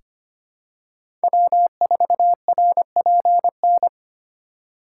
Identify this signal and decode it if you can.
Morse code, heard as a keyed tone, about 25 wpm, 700 Hz. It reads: W4RPN
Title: USA Call Signs - 2x Character Spacing - 25wpm